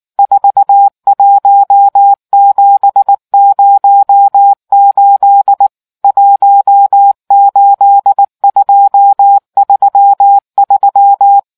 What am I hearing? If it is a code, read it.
Morse code, heard as a keyed tone, about 19 wpm, 800 Hz. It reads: 41708 18233